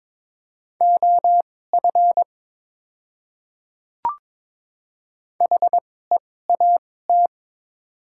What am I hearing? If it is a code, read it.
Morse code, heard as a keyed tone, about 22 wpm, spaced as for 16 wpm, 700 Hz. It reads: OF HEAT
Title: 2x Character Spacing - Top 500 Words - 22wpm